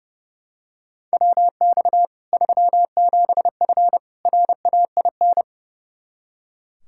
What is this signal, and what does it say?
Morse code, heard as a keyed tone, about 32 wpm, 700 Hz. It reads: WX 37F RAIN
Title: Practice QSO - Exchange 2 - Info - 30wpm